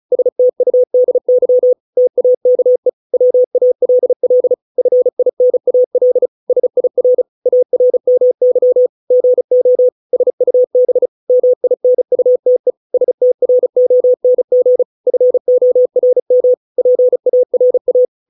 Morse code, heard as a keyed tone, about 34 wpm, 500 Hz. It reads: STUDY TAKE WALL FINAL SIR ARMY GO SUB MINUTE STRONG FORM PARA